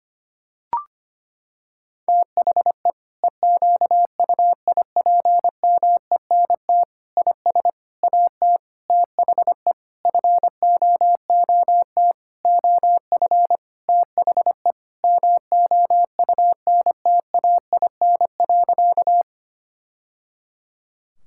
Morse code, heard as a keyed tone, about 25 wpm, 700 Hz. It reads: THE EQUIPMENT IS AT THE FOOT OF THE MOUNTAIN.
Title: Sentences from Top 1000 Words - Review Part 5 - 25wpm